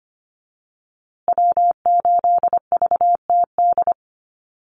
Morse code, heard as a keyed tone, about 25 wpm, 700 Hz. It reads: W8VTD
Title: USA Call Signs Repeated 3x 25wpm